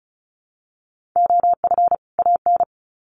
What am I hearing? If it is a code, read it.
Morse code, heard as a keyed tone, about 34 wpm, 700 Hz. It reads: OF AN
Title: Sets of 2 Words 35wpm